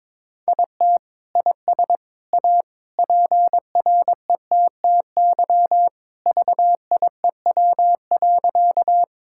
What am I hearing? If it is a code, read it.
Morse code, heard as a keyed tone, about 21 wpm, 700 Hz. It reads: IT IS A PRETTY VIEW.
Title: Sentences from Top 1000 Words - Review Part 6 - 22wpm